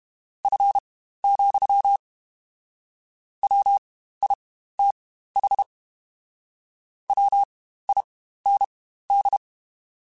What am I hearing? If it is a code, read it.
Morse code, heard as a keyed tone, about 32 wpm, spaced as for 15 wpm, 800 Hz. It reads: F, WITH WIND
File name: Spaced Characters